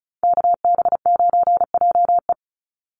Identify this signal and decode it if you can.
Morse code, heard as a keyed tone, about 34 wpm, 700 Hz. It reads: KB9JE
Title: USA Call Signs 35wpm